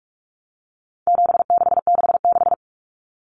Morse code, heard as a keyed tone, about 44 wpm, 700 Hz. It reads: 7666
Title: Numbers - 4-Digit 45wpm